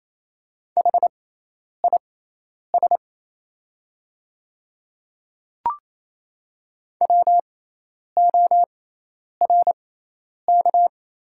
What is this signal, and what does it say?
Morse code, heard as a keyed tone, about 28 wpm, spaced as for 10 wpm, 700 Hz. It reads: HIS WORK
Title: 6x Character Spacing - Top 200 Words - 28wpm